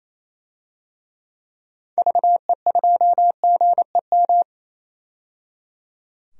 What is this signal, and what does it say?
Morse code, heard as a keyed tone, about 28 wpm, 700 Hz. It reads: VE2GEM
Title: DXCC-100 Call Signs - 6 Characters - 28wpm